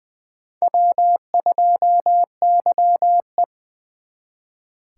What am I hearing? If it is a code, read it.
Morse code, heard as a keyed tone, about 23 wpm, 700 Hz. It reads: W2YE
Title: Speed Racing - USA 1x2 Call Signs 20wpm